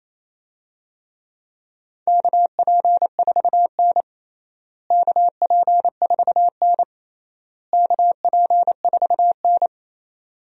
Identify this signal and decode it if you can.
Morse code, heard as a keyed tone, about 28 wpm, 700 Hz. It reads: KP4N KP4N KP4N
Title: DXCC-100 Call Signs - 4 Characters - Repeated 3x - 28wpm